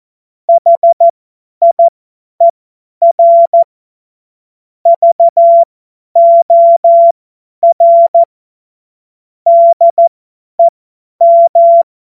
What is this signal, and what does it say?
Morse code, heard as a keyed tone, about 14 wpm, spaced as for 10 wpm, 680 Hz. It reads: HIER VOR DEM
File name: Morsekurs / Witze / Witz 27. September 2022 50 F70